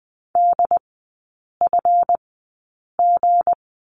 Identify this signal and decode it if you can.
Morse code, heard as a keyed tone, about 20 wpm, spaced as for 8 wpm, 700 Hz. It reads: DFG
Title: 178 Lesson 29e - Slash - Three Characters 20wpm